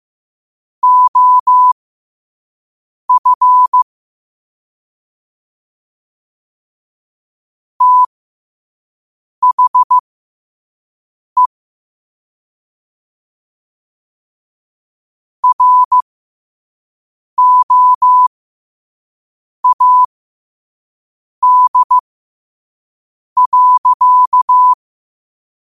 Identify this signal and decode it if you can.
Morse code, heard as a keyed tone, about 15 wpm, spaced as for 5 wpm, 1000 Hz. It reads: OF THE ROAD.
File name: qotdfiles / 5wpm